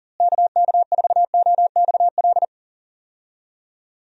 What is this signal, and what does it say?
Morse code, heard as a keyed tone, about 40 wpm, 700 Hz. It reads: KK4OXL
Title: USA Call Signs - 6 Characters - Repeated 2x - 40wpm